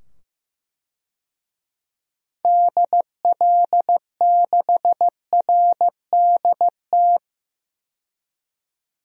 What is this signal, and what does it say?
Morse code, heard as a keyed tone, about 15 wpm, 700 Hz. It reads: DL6RDT
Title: DXCC-100 Call Signs - 6 Characters - 15wpm